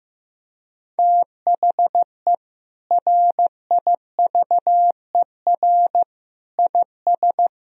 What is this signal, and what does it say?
Morse code, heard as a keyed tone, about 15 wpm, 700 Hz. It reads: THE RIVER IS
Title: Sentences from Top 1000 Words - Review Part 1 - 15wpm